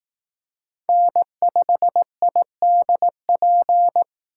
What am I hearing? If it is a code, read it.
Morse code, heard as a keyed tone, about 18 wpm, 700 Hz. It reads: N5IDP
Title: Speed Racing - USA 1x3 Call Signs 15wpm